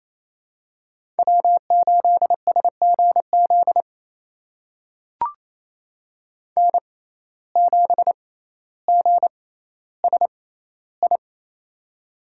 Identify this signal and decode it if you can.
Morse code, heard as a keyed tone, about 28 wpm, spaced as for 14 wpm, 700 Hz. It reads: W8SGZ N7GSI
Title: USA Call Signs - 5 Characters - 6x Character Spacing - 28wpm